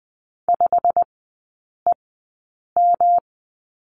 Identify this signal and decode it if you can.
Morse code, heard as a keyed tone, about 20 wpm, spaced as for 8 wpm, 700 Hz. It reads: 5EM
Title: 107 Lesson 18e - Letter m - Three Characters 20wpm